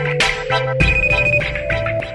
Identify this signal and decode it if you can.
There is no code.